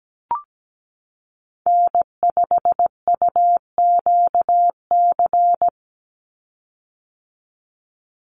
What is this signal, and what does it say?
Morse code, heard as a keyed tone, about 17 wpm, 700 Hz. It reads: N5UQC
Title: USA Call Signs 17wpm